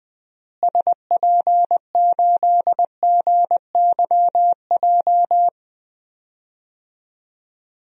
Morse code, heard as a keyed tone, about 20 wpm, 700 Hz. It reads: SP8GYJ
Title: DXCC-100 Call Signs - 6 Characters - Repeated 2x - 20wpm